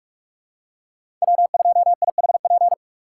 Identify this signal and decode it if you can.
Morse code, heard as a keyed tone, about 41 wpm, 700 Hz. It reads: W2IHP
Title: USA Call Signs - 2x Character Spacing - 45wpm